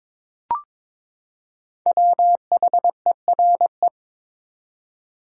Morse code, heard as a keyed tone, about 22 wpm, 700 Hz. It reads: WHERE
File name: Build A Phrase - 2-Word Phrases - 22wpm